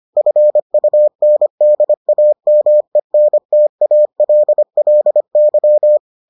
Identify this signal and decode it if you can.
Morse code, heard as a keyed tone, about 25 wpm, 600 Hz. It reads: FUNDAMENTALLY